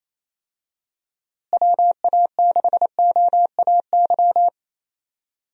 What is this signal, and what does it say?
Morse code, heard as a keyed tone, about 28 wpm, 700 Hz. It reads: WA6OAY
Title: USA Call Signs - 6 Characters - Repeated 2x - 28wpm